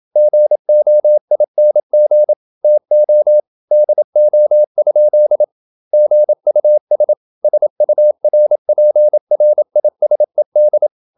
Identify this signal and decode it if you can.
Morse code, heard as a keyed tone, about 27 wpm, 600 Hz. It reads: GOING TO DO? GUS SURPRISED
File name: Never 0027wpm